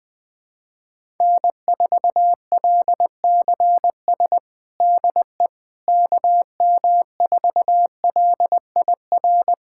Morse code, heard as a keyed tone, about 21 wpm, 700 Hz. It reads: N4LCS DE KM4LIR
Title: USA Callsign DE Callsign 20wpm